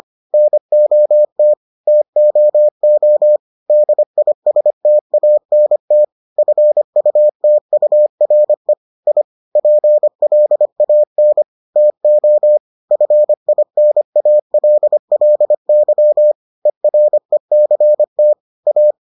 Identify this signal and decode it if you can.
Morse code, heard as a keyed tone, about 25 wpm, 600 Hz. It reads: NOT TOO DISTANT FUTURE I PLAN TO FINALLY ERECT A